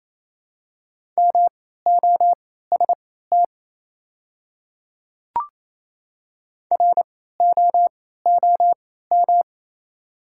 Morse code, heard as a keyed tone, about 28 wpm, spaced as for 16 wpm, 700 Hz. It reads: MOST ROOM
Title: 3x Character Spacing - Top 300 Words - 28wpm